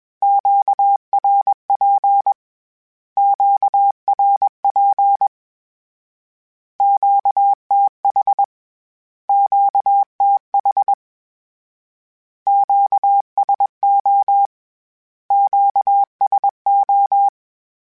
Morse code, heard as a keyed tone, about 21 wpm, 800 Hz. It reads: QRP QRP QTH QTH QSO QSO
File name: Q Calls / Q Calls-mp3s / Q Calls Twice @20